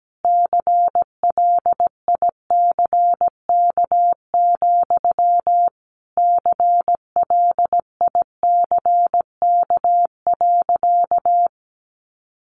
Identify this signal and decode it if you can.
Morse code, heard as a keyed tone, about 17 wpm, 700 Hz. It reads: CLICK, CLICK.
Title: Sentences from Top 1100 Words 17wpm